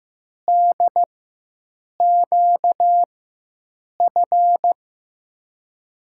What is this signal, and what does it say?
Morse code, heard as a keyed tone, about 15 wpm, spaced as for 7 wpm, 700 Hz. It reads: DQF
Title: USA Call Signs - 6 Characters - 4x Character Spacing - 15wpm